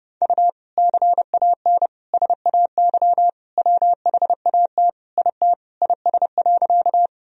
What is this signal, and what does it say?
Morse code, heard as a keyed tone, about 30 wpm, 700 Hz. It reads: U CAN SAY WHAT IT IS.
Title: Sentences from Top 200 Words and Abbreviations 30wpm